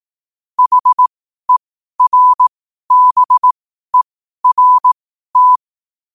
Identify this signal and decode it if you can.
Morse code, heard as a keyed tone, about 18 wpm, spaced as for 12 wpm, 1000 Hz. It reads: HERBERT
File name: qotdfiles / 12wpm